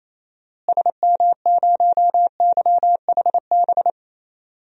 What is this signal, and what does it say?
Morse code, heard as a keyed tone, about 28 wpm, 700 Hz. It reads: SM0YHB
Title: DXCC-100 Call Signs - 6 Characters - Repeated 2x - 28wpm